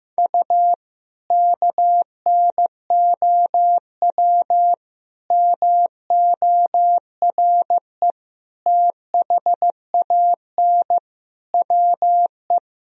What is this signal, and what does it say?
Morse code, heard as a keyed tone, about 16 wpm, 700 Hz. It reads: U KNOW MORE THAN WE
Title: Sentences from Top 100 Words and Abbreviations 15wpm